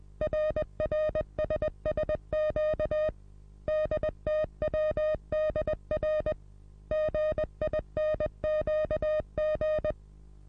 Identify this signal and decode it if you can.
Morse code, heard as a keyed tone, about 20 wpm, 620 Hz. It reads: RRSSQ DTWDR GINQG
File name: repos / Morsekurs / DJ4UF Morsen27